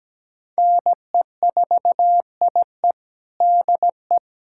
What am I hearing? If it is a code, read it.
Morse code, heard as a keyed tone, about 17 wpm, 700 Hz. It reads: NE4IE DE
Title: USA Callsign DE Callsign 17wpm